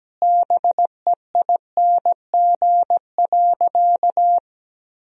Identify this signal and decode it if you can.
Morse code, heard as a keyed tone, about 17 wpm, 700 Hz. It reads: BEING.
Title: Sentences from Top 1000 Words - Review Part 4 - 17wpm